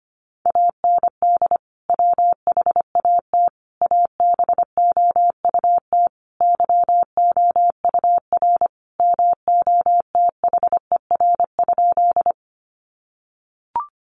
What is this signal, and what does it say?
Morse code, heard as a keyed tone, about 25 wpm, 700 Hz. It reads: AND WHAT ABOUT YOUR MOTHER?
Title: Sentences from Top 200 Words 25wpm